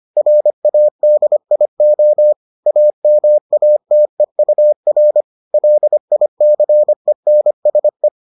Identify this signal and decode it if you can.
Morse code, heard as a keyed tone, about 25 wpm, 600 Hz. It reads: RADIO AMATEUR LICENSE